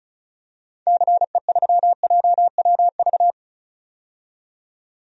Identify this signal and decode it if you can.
Morse code, heard as a keyed tone, about 36 wpm, 700 Hz. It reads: CE3JWV
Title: DXCC-100 Call Signs - 6 Characters - 35wpm